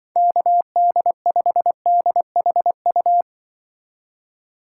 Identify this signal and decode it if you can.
Morse code, heard as a keyed tone, about 23 wpm, 700 Hz. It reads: KD5DHU
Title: Speed Racing - USA Call Signs 20wpm